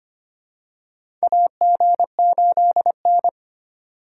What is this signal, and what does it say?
Morse code, heard as a keyed tone, about 25 wpm, 700 Hz. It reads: AG8N
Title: USA Call Signs - 4 Characters - 3x Character Spacing - 25wpm